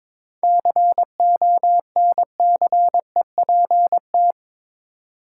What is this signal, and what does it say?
Morse code, heard as a keyed tone, about 22 wpm, 700 Hz. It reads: CONCEPT
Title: Seven Letter Words - 22wpm